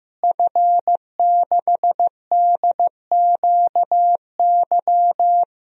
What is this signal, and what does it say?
Morse code, heard as a keyed tone, about 15 wpm, 700 Hz. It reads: F6DQY
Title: DXCC-100 Call Signs - 5 Characters - Repeated 3x - 15wpm